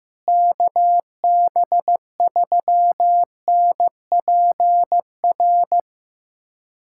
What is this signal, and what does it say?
Morse code, heard as a keyed tone, about 15 wpm, 700 Hz. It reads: KB3NPR
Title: USA Callsign DE Callsign - 6 Characters - 15wpm